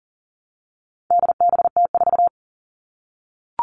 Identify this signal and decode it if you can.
Morse code, heard as a keyed tone, about 40 wpm, 700 Hz. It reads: DBT4
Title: All Cut Numbers - 4-Digit 40wpm